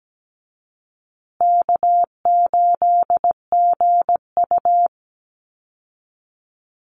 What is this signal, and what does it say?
Morse code, heard as a keyed tone, about 17 wpm, 700 Hz. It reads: K8GU
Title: Easy Call Signs Repeated 2x - 17wpm